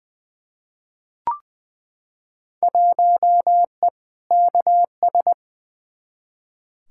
Morse code, heard as a keyed tone, about 20 wpm, 700 Hz. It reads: 1E KS
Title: Field Day Exchanges - 20wpm